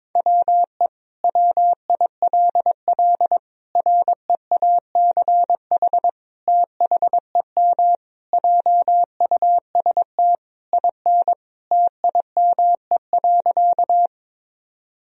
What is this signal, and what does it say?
Morse code, heard as a keyed tone, about 22 wpm, 700 Hz. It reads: WE WILL REACH THEM JUST IN TIME.
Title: Sentences from Top 1000 Words - Review Part 2 - 22wpm